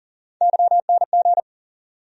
Morse code, heard as a keyed tone, about 40 wpm, 700 Hz. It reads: YNG
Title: USA Call Sign Postfixes - 3 Letters 40wpm